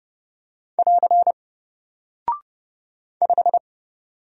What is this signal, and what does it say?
Morse code, heard as a keyed tone, about 30 wpm, 700 Hz. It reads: <AR> 5
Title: Single Character (Letter:Number:Punctuation:Prosign) Alt 30wpm